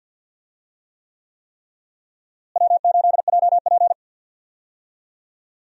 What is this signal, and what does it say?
Morse code, heard as a keyed tone, about 50 wpm, 700 Hz. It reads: W8JP
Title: USA Call Signs - 4 Characters - Repeated 3x - 50wpm